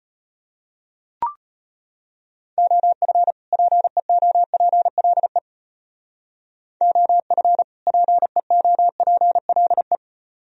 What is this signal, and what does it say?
Morse code, heard as a keyed tone, about 32 wpm, 700 Hz. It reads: OF PEOPLE OF PEOPLE